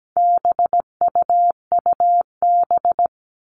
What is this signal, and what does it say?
Morse code, heard as a keyed tone, about 17 wpm, 700 Hz. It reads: BUUB
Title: All Cut Numbers - 4-Digit 17wpm